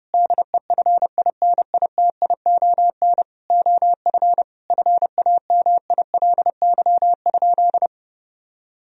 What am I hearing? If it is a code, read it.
Morse code, heard as a keyed tone, about 30 wpm, 700 Hz. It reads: DEFINITION OF FAMILY?
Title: Sentences from Top 1000 Words - Review Part 4 - 30wpm